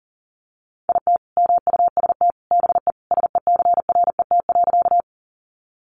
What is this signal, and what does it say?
Morse code, heard as a keyed tone, about 40 wpm, 700 Hz. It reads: IT MUST BE SECRET.
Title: Sentences from Top 1200 Words 40wpm